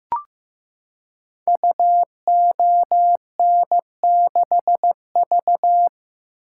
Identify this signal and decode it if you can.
Morse code, heard as a keyed tone, about 15 wpm, 700 Hz. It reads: UON6V
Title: Code Groups - 5 Letter:Numbers - 15wpm